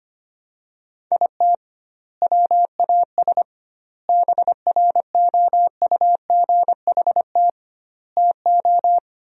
Morse code, heard as a keyed tone, about 25 wpm, 700 Hz. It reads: IT WAS BROUGHT TO
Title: 2x Word-Spacing - Sentences from Top 500 Words - 25wpm